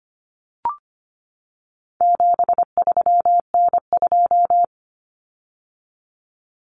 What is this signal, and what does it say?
Morse code, heard as a keyed tone, about 25 wpm, 700 Hz. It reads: 73N2
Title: Common Cut Numbers - 4-Digit 25wpm